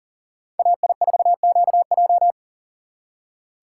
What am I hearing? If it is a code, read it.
Morse code, heard as a keyed tone, about 52 wpm, 700 Hz. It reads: AI4QJ